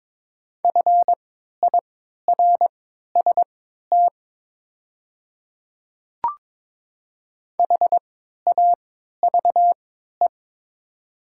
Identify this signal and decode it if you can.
Morse code, heard as a keyed tone, about 22 wpm, spaced as for 12 wpm, 700 Hz. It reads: FIRST HAVE
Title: 3x Character Spacing - Top 200 Words - 22wpm